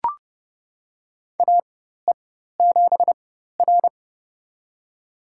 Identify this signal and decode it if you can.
Morse code, heard as a keyed tone, about 29 wpm, spaced as for 14 wpm, 700 Hz. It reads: AE7R
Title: USA Call Signs - 4 Characters - 4x Character Spacing - 30wpm